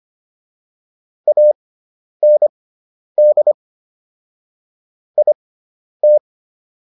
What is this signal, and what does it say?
Morse code, heard as a keyed tone, about 25 wpm, spaced as for 10 wpm, 600 Hz. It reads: AND IT